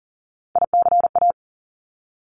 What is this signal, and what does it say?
Morse code, heard as a keyed tone, about 41 wpm, 690 Hz. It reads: ICA
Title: Top Three Letter Combinations in Words 40wpm